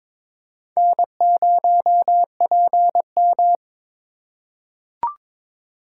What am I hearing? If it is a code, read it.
Morse code, heard as a keyed tone, about 22 wpm, 700 Hz. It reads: N0PM
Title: USA Call Signs - 4 Characters - 2x Character Spacing - 22wpm